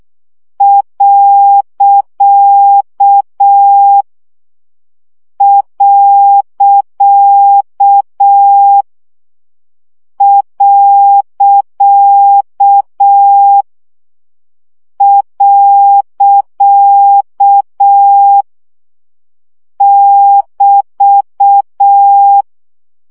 Morse code, heard as a keyed tone, about 6 wpm, 800 Hz. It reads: . . <BT>
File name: CW gyakorlas pont 5 baud